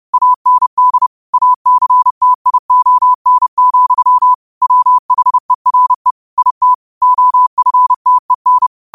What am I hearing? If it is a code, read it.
Morse code, heard as a keyed tone, about 30 wpm, 1000 Hz. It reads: AND ACTION, WHERE IT OFTEN
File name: qotdfiles / 30wpm